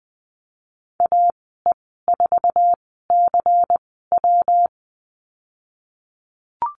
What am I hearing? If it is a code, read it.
Morse code, heard as a keyed tone, about 20 wpm, spaced as for 15 wpm, 700 Hz. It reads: AE4CW
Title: 128 Lesson 21g - Number 6 - Call Signs 20wpm